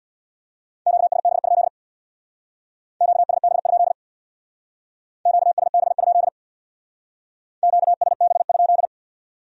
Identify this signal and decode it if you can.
Morse code, heard as a keyed tone, about 63 wpm, 700 Hz. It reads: QSB? QSB? QSB? QSB?